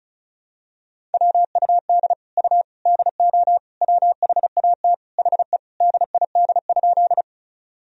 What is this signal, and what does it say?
Morse code, heard as a keyed tone, about 36 wpm, 700 Hz. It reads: WUD U DO WHAT HE DID?
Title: Sentences from Top 100 Words and Abbreviations 35wpm